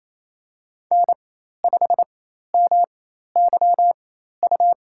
Morse code, heard as a keyed tone, about 28 wpm, spaced as for 13 wpm, 700 Hz. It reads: N5MYU
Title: USA Call Signs - 5 Characters - 4x Character Spacing - 28wpm